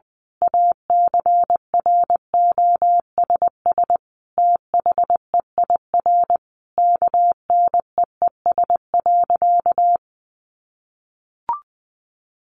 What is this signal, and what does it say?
Morse code, heard as a keyed tone, about 20 wpm, 700 Hz. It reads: ACROSS THEIR KNEES.
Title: A Sound Of Thunder 20wpm